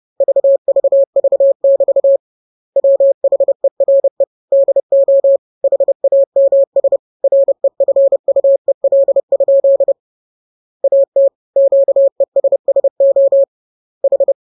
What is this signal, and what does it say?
Morse code, heard as a keyed tone, about 30 wpm, 550 Hz. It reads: VVV<BT> WHERE DO HAMS REFUEL? AT QESSO H
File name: jokes / joke 22 30 0000